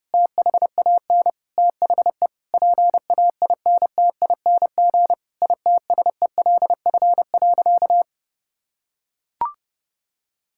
Morse code, heard as a keyed tone, about 32 wpm, 700 Hz. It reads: THAN THE PAINTING ITSELF.
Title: Sentences from Top 1000 Words - Review Part 6 - 30wpm